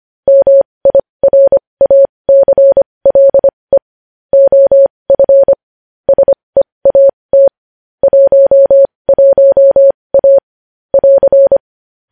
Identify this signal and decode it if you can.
Morse code, heard as a keyed tone, about 25 wpm, spaced as for 20 wpm, 550 Hz. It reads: MIRACLE OF SEAT 11A <AR>
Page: News headlines in Morse code at 20 wpm